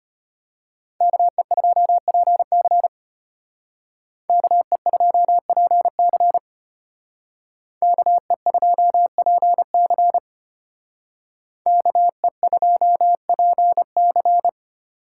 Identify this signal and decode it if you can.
Morse code, heard as a keyed tone, about 32 wpm, 700 Hz. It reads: KE2PC KE2PC KE2PC KE2PC